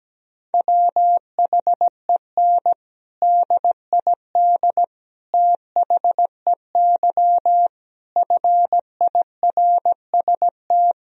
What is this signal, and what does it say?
Morse code, heard as a keyed tone, about 17 wpm, 700 Hz. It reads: WHEN DID THEY FIRST
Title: Sentences from Top 1000 Words - Review Part 2 - 17wpm